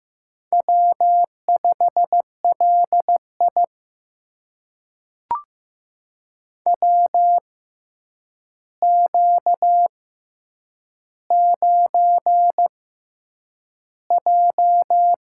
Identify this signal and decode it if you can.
Morse code, heard as a keyed tone, about 15 wpm, spaced as for 8 wpm, 700 Hz. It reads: W5LI WQ9J
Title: USA Call Signs - 6x Character Spacing - 15wpm